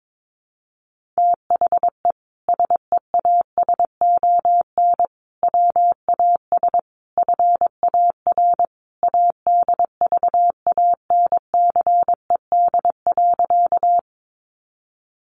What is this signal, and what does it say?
Morse code, heard as a keyed tone, about 22 wpm, 700 Hz. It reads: THE SEASON WAS FAR ADVANCED.
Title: Sentences from Top 1400 Words 22wpm